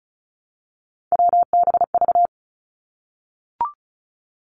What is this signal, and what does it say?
Morse code, heard as a keyed tone, about 36 wpm, 700 Hz. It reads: WBV
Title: Three Challenging Letter-Number Combinations 35wpm